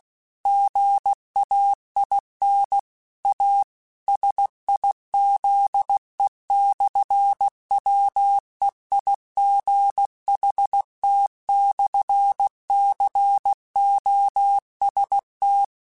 Morse code, heard as a keyed tone, about 16 wpm, 800 Hz. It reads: GAIN A SIZE/WEIGHT/COST
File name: Technical Subjects / Technical Subjects-mp3s / Sw Pwr Sup @15